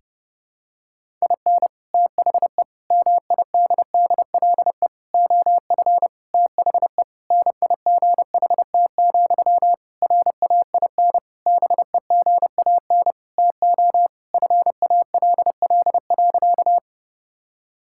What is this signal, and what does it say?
Morse code, heard as a keyed tone, about 30 wpm, 700 Hz. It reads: IN THE MIDDLE OF THE NIGHT, RAIN BEGAN TO FALL.
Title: Sentences from Top 1000 Words - Review Part 6 - 30wpm